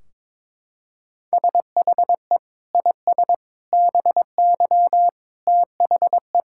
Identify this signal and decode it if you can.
Morse code, heard as a keyed tone, about 22 wpm, 700 Hz. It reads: SHE IS BY THE
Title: Sentences from Top 1600 Words 22wpm